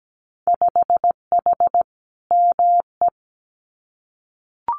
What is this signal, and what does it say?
Morse code, heard as a keyed tone, about 17 wpm, 700 Hz. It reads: 5H ME
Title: Winter Field Day Exchanges Encore 17wpm